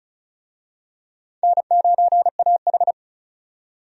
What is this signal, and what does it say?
Morse code, heard as a keyed tone, about 36 wpm, 700 Hz. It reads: N9AH
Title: USA Call Signs - 4 Characters - 35wpm